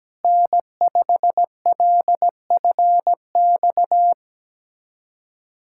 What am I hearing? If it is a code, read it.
Morse code, heard as a keyed tone, about 17 wpm, 700 Hz. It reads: N5LFX
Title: USA Callsign DE Callsign 17wpm